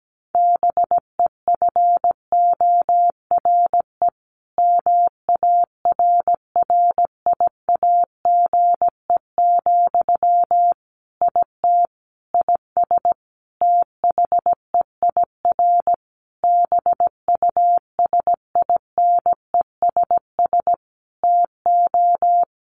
Morse code, heard as a keyed tone, about 17 wpm, 700 Hz. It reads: BEFORE MARRIAGE, IT IS THEIR BUSINESS TO
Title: Sentences from Top 1500 Words 17wpm